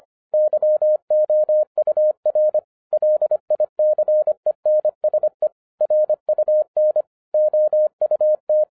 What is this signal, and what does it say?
Morse code, heard as a keyed tone, about 25 wpm, 610 Hz. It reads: YOUR LICENSE RUN OUT